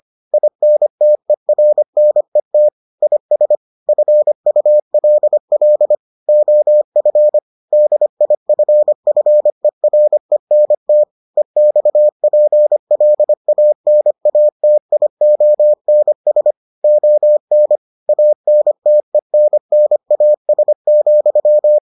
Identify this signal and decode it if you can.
Morse code, heard as a keyed tone, about 25 wpm, 600 Hz. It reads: INTERNET IS FULL OF DIFFERENT EXPLANATIONS ON ANTENNAS,